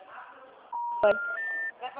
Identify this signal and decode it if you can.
Special information tone, with speech.
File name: out- -2004- - - .58267